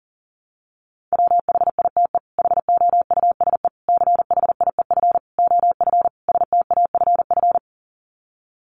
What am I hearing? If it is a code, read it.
Morse code, heard as a keyed tone, about 40 wpm, 700 Hz. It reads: WHITE HOUSE CHIEF OF STAFF
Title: Sets of 5 Words Encore 40wpm